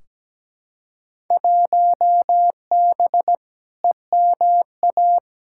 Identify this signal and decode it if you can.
Morse code, heard as a keyed tone, about 17 wpm, 700 Hz. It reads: 1B EMA
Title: Field Day Exchanges - 17wpm